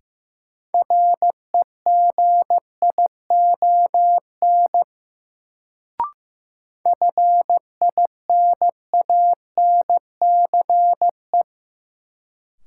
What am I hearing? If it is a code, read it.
Morse code, heard as a keyed tone, about 16 wpm, 700 Hz. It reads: REGION FINANCE
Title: Sentences from Top 1500 Words - New Word Drill - 15wpm